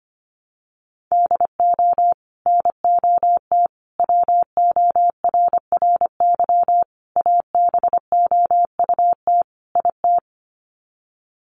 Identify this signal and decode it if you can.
Morse code, heard as a keyed tone, about 25 wpm, 700 Hz. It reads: DO NOT WORRY ABOUT IT
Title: Sets of 5 Words 25wpm